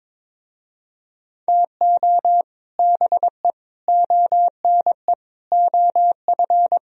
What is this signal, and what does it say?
Morse code, heard as a keyed tone, about 22 wpm, 700 Hz. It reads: TO BE ONE OF
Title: Speed-Racing - Sets of 5 Words 22wpm